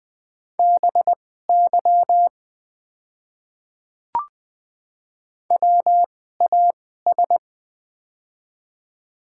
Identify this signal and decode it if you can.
Morse code, heard as a keyed tone, about 20 wpm, spaced as for 15 wpm, 700 Hz. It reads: BY WAS
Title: 2x Character Spacing - Top 100 Words - 20wpm